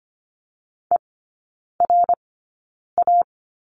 Morse code, heard as a keyed tone, about 25 wpm, spaced as for 9 wpm, 700 Hz. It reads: ERA
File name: Lesson 15e - Number 5 - Three Characters 25wpm